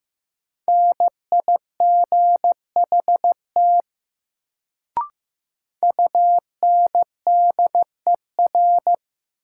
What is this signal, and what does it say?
Morse code, heard as a keyed tone, about 15 wpm, 700 Hz. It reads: NIGHT UNDER
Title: Sentences from Top 300 Words - New Word Drill - 15wpm